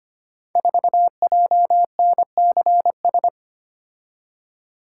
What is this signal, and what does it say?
Morse code, heard as a keyed tone, about 25 wpm, 700 Hz. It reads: 4JNCS
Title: Code Groups - 5 Letter:Numbers - 25wpm